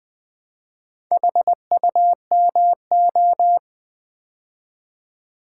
Morse code, heard as a keyed tone, about 20 wpm, 700 Hz. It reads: HUMO
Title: Build A Word - 2 Letters At A Time - 6-Letter Words - 20wpm